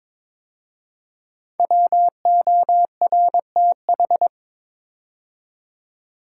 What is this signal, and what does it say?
Morse code, heard as a keyed tone, about 22 wpm, 700 Hz. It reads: WORTH
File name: 5 Letter Words - First 3 Letters Given - 22wpm